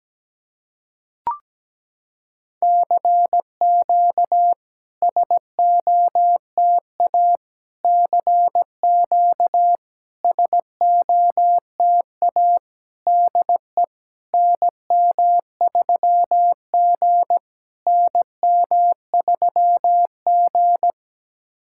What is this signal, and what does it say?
Morse code, heard as a keyed tone, about 17 wpm, 700 Hz. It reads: CQ SOTA CQ SOTA DE NM3G NM3G
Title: SOTA QSO Practice 17wpm